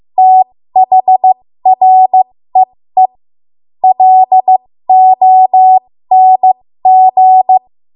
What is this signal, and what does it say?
Morse code, heard as a keyed tone, about 15 wpm, spaced as for 13 wpm, 750 Hz. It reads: THREE LONG
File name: Morse / 13WPM